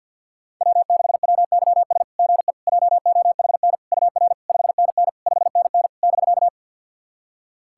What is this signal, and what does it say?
Morse code, heard as a keyed tone, about 50 wpm, 690 Hz. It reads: W6WYS GE JOHN UR 5NN 5NN <BK>
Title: SOTA QSO Practice 50wpm